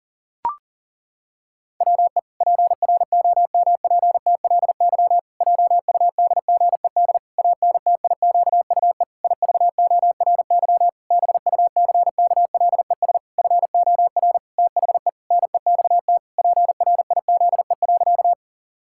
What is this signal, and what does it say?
Morse code, heard as a keyed tone, about 40 wpm, 700 Hz. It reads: WE PROMPTLY JUDGED ANTIQUE IVORY BUCKLES FOR THE NEXT PRIZE.
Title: Pangram Challenge v2 - 40wpm